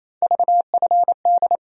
Morse code, heard as a keyed tone, about 28 wpm, 700 Hz. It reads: VFD
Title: USA Call Sign Postfixes - 3 Letters 28wpm